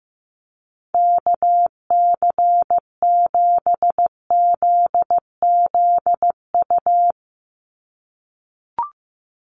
Morse code, heard as a keyed tone, about 15 wpm, 700 Hz. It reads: KC7ZZU
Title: USA Call Signs 15wpm